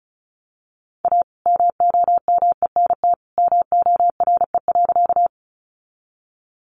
Morse code, heard as a keyed tone, about 34 wpm, 700 Hz. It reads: A MOMENT MORE.
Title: Sentences from Top 1000 Words 35wpm